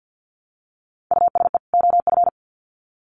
Morse code, heard as a keyed tone, about 50 wpm, 700 Hz. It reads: USE OF
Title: Sets of 2 Words 50wpm